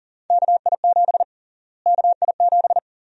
Morse code, heard as a keyed tone, about 40 wpm, 700 Hz. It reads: KI7 KI7
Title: USA Call Sign Prefixes - Repeated 2x - 40wpm